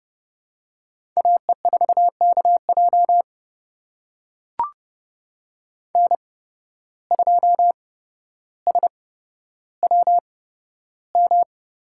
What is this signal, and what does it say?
Morse code, heard as a keyed tone, about 29 wpm, spaced as for 13 wpm, 700 Hz. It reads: AE4KJ N2SWM
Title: USA Call Signs - 5 Characters - 8x Character Spacing - 30wpm